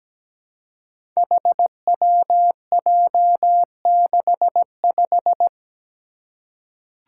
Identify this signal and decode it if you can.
Morse code, heard as a keyed tone, about 17 wpm, 700 Hz. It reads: HWJ65
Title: Five Challenging Letter-Number Combinations 17wpm